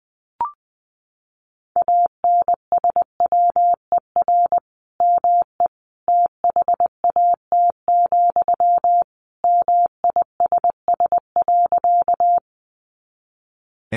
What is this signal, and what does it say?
Morse code, heard as a keyed tone, about 20 wpm, 700 Hz. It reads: ANSWER ME THAT, MISS.
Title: Sentences from Top 500 Words 20wpm